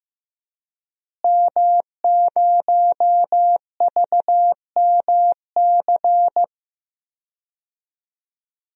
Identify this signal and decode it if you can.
Morse code, heard as a keyed tone, about 15 wpm, 700 Hz. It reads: M0VMC
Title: DXCC-100 Call Signs - 5 Characters - Repeated 2x - 15wpm